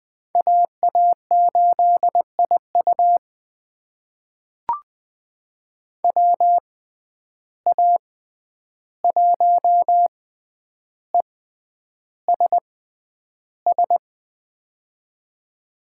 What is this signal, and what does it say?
Morse code, heard as a keyed tone, about 21 wpm, spaced as for 10 wpm, 700 Hz. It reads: AA8IU WA1ESS
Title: USA Call Signs - 6x Character Spacing - 20wpm